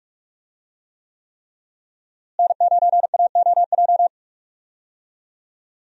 Morse code, heard as a keyed tone, about 46 wpm, 700 Hz. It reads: N9AOJ
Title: USA Call Signs - 5 Characters - Repeated 3x - 45wpm